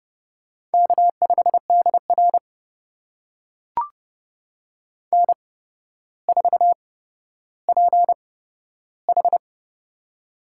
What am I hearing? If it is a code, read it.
Morse code, heard as a keyed tone, about 32 wpm, spaced as for 13 wpm, 700 Hz. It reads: K5DR N4PH
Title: USA Call Signs - 4 Characters - 8x Character Spacing - 30wpm